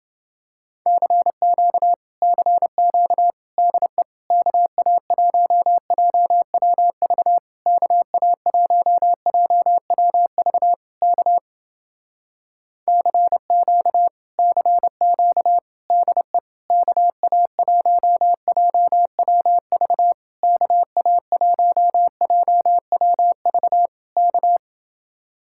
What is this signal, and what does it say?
Morse code, heard as a keyed tone, about 29 wpm, 700 Hz. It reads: CQ CQ DE KA1JWV KA1JWV K CQ CQ DE KA1JWV KA1JWV K